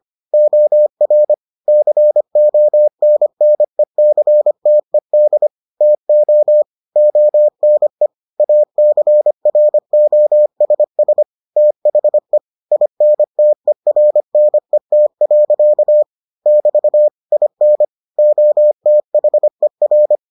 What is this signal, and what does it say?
Morse code, heard as a keyed tone, about 25 wpm, 600 Hz. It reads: OR CONNECTED TO ONE ACROSS THE INTERNET. <BT> IN OTHER